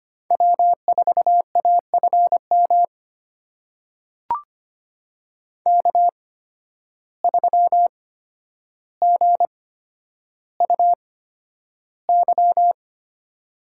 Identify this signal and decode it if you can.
Morse code, heard as a keyed tone, about 25 wpm, spaced as for 11 wpm, 700 Hz. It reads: W4AFM K3GUY
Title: USA Call Signs - 5 Characters - 8x Character Spacing - 25wpm